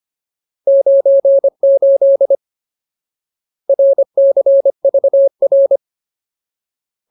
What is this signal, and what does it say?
Morse code, heard as a keyed tone, about 25 wpm, 550 Hz. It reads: 98 RCVR
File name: Ultimate Word List / 25WPM Ultimate